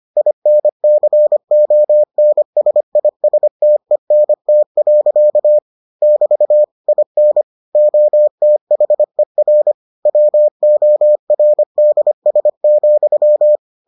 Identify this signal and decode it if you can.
Morse code, heard as a keyed tone, about 25 wpm, 600 Hz. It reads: INCONSISTENT. <BT> IN OTHER WORDS,